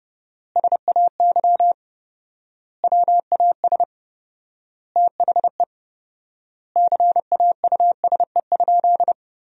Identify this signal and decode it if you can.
Morse code, heard as a keyed tone, about 30 wpm, 700 Hz. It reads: SAY WAS THE CAUSE?
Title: 4x Word-Spacing - Sentences from Top 200 Words - 30wpm